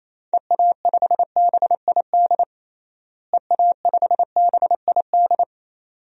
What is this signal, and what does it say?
Morse code, heard as a keyed tone, about 28 wpm, 700 Hz. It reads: EA5BID EA5BID
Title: DXCC-100 Call Signs - 6 Characters - Repeated 2x - 28wpm